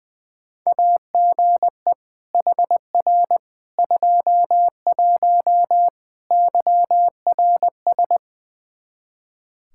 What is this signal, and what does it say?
Morse code, heard as a keyed tone, about 21 wpm, 700 Hz. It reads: AGE HR 21 YRS
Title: K1USN SST QSO Practice 20wpm